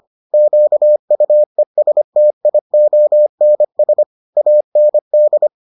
Morse code, heard as a keyed tone, about 25 wpm, 600 Hz. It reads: QUESTIONS AND